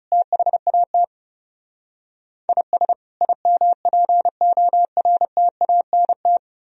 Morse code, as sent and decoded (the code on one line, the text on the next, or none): - .... .- - / .. ... / .. -- .--. --- .-. - .- -. -
THAT IS IMPORTANT